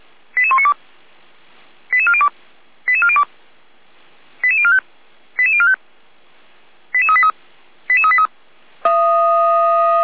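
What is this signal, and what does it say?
Five-tone selective call (ZVEI), heard as quick runs of five tones, tones of about 80 ms. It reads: 88192 88402 88402 88045 88045 88383 88383
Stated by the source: ZVEI five-tone call 88102 on channel 455, received by SDR at Feuerwehr Orlamünde fire service.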